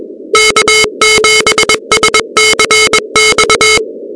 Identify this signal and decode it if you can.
Morse code, heard as a keyed tone, about 21 wpm, 2150 Hz, with noise. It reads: K7SCX
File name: contests / 2019-cw-wpx-kt5j-k5pi- -2330 0-13